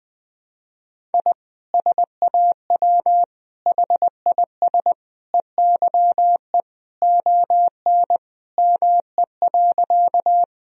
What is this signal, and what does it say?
Morse code, heard as a keyed tone, about 20 wpm, 700 Hz. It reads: I SAW HIS EYE ON ME.
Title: Sentences from Top 1000 Words - Review Part 1 - 20wpm